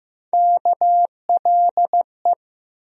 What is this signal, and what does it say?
Morse code, heard as a keyed tone, about 15 wpm, 700 Hz. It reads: KLE
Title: USA Call Sign Postfixes - 3 Letters 15wpm